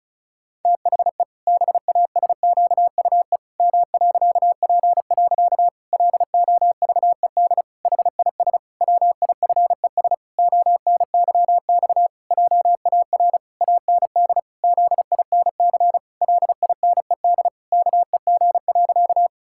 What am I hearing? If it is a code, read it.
Morse code, heard as a keyed tone, about 34 wpm, 700 Hz. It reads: THE BASQUE M.P. LOVED HIS WIFES ONYX JAR AND ZINC LINED KEG.
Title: Pangram Challenge v2 - 35wpm